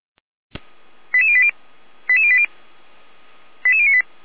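Five-tone selective call (ZVEI), heard as quick runs of five tones, tones of about 70 ms. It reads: 88080 88080 88098
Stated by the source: ZVEI five-tone selective call received by SDR.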